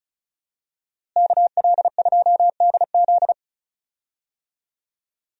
Morse code, heard as a keyed tone, about 34 wpm, 700 Hz. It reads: KL2DZ
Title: USA Call Signs - 5 Characters - Repeated 2x - 35wpm